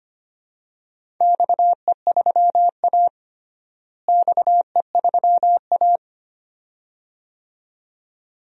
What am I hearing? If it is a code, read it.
Morse code, heard as a keyed tone, about 25 wpm, 700 Hz. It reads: XE3A XE3A
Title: DXCC-100 Call Signs - 4 Characters - Repeated 2x - 25wpm